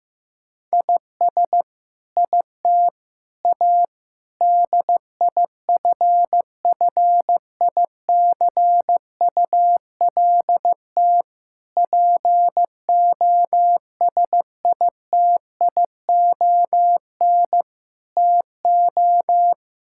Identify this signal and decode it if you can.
Morse code, heard as a keyed tone, about 15 wpm, 700 Hz. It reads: IS IT A DIFFICULT POSITION TO
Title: Sentences from Top 1000 Words - Review Part 5 - 15wpm